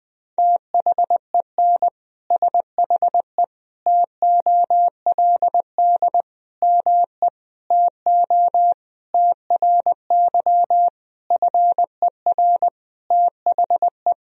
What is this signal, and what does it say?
Morse code, heard as a keyed tone, about 19 wpm, 700 Hz. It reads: THEN SHE TOLD ME TO TRY FER THE
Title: Sentences from Top 400 Words and Abbreviations 20wpm